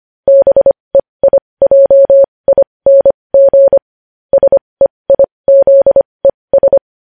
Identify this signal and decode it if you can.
Morse code, heard as a keyed tone, about 25 wpm, spaced as for 20 wpm, 550 Hz. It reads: BEIJING SEIZES
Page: news headlines in morse code at 20 wpm